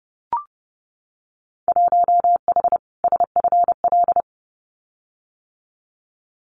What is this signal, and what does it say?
Morse code, heard as a keyed tone, about 30 wpm, 700 Hz. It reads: 1H SFL
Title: Winter Field Day Exchanges 30wpm